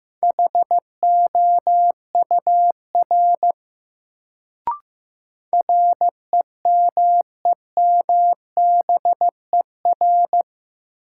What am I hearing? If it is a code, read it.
Morse code, heard as a keyed tone, about 16 wpm, 700 Hz. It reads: HOUR REMEMBER
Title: Sentences from Top 400 Words - New Word Drill - 15wpm